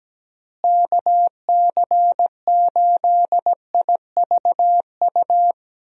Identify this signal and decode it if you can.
Morse code, heard as a keyed tone, about 17 wpm, 700 Hz. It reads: KC8IVU
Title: USA Call Signs - 6 Characters - Repeated 3x - 17wpm